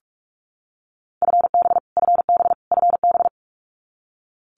Morse code, heard as a keyed tone, about 44 wpm, 700 Hz. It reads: FB FB FB
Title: Abbreviations Repeated 3x - 45wpm